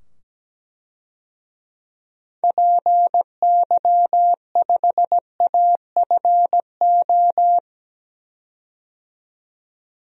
Morse code, heard as a keyed tone, about 17 wpm, 700 Hz. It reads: PY5AFO
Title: DXCC-100 Call Signs - 6 Characters - 17wpm